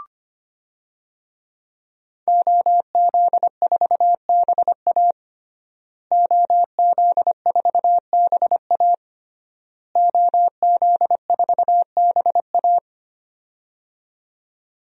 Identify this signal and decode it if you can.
Morse code, heard as a keyed tone, about 25 wpm, 700 Hz. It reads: OZ4BA OZ4BA OZ4BA
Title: DXCC-100 Call Signs - 5 Characters - Repeated 3x - 25wpm